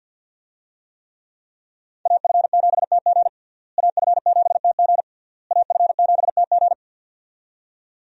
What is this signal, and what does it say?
Morse code, heard as a keyed tone, about 47 wpm, 700 Hz. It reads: AF7TG AF7TG AF7TG
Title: USA Call Signs - 5 Characters - Repeated 3x - 50wpm